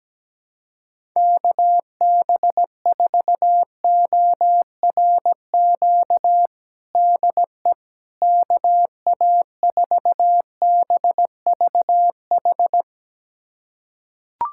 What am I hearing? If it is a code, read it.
Morse code, heard as a keyed tone, about 17 wpm, 700 Hz. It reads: KB4ORQ DE KA4BVH
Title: USA Callsign DE Callsign - 6 Characters - 17wpm